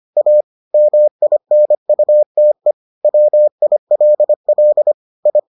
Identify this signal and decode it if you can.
Morse code, heard as a keyed tone, about 25 wpm, 600 Hz. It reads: A MINUTE WILL I